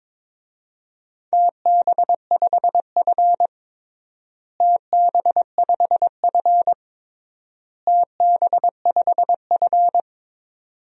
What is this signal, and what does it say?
Morse code, heard as a keyed tone, about 22 wpm, 700 Hz. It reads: TB5F TB5F TB5F
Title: DXCC-100 Call Signs - 4 Characters - Repeated 3x - 22wpm